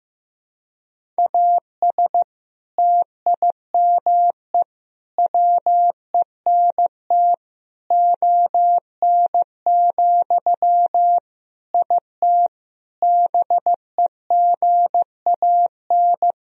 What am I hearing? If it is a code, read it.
Morse code, heard as a keyed tone, about 15 wpm, 700 Hz. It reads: AS TIME WENT ON, IT BEGAN
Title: Sentences from Top 1000 Words - Review Part 1 - 15wpm